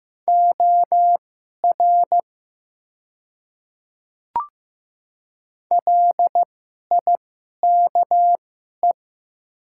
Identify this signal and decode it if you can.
Morse code, heard as a keyed tone, about 15 wpm, spaced as for 11 wpm, 700 Hz. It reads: OR LIKE
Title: 2x Character Spacing - Top 200 Words - 15wpm